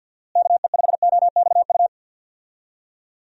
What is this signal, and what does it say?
Morse code, heard as a keyed tone, about 46 wpm, 700 Hz. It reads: KE5OXU
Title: USA Call Signs - 6 Characters - Repeated 3x - 50wpm